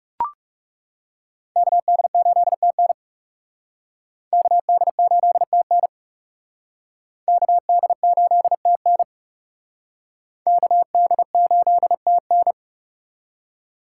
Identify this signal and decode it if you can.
Morse code, heard as a keyed tone, about 39 wpm, 700 Hz. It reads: KD8TN KD8TN KD8TN KD8TN